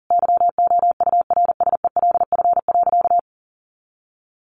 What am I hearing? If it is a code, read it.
Morse code, heard as a keyed tone, about 40 wpm, 700 Hz. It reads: YOURSELF.
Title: Sentences from Top 600 Words 40wpm